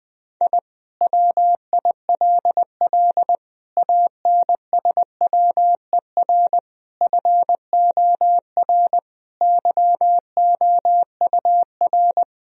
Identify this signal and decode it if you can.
Morse code, heard as a keyed tone, about 20 wpm, 700 Hz. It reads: I WILL ANSWER FOR YOUR
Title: Sentences from Top 1000 Words - Review Part 4 - 20wpm